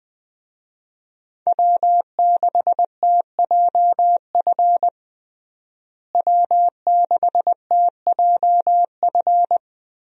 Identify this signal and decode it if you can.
Morse code, heard as a keyed tone, about 20 wpm, 700 Hz. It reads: W6TJF W6TJF
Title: USA Call Signs - 5 Characters - Repeated 3x - 20wpm